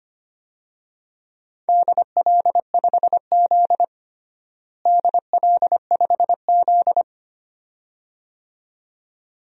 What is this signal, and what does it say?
Morse code, heard as a keyed tone, about 25 wpm, 700 Hz. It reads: DL5Z DL5Z
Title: DXCC-100 Call Signs - 4 Characters - Repeated 2x - 25wpm